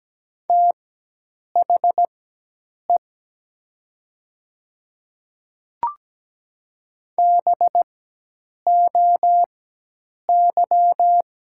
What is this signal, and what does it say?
Morse code, heard as a keyed tone, about 17 wpm, spaced as for 8 wpm, 700 Hz. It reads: THE BOY
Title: 4x Character Spacing - Top 300 Words - 17wpm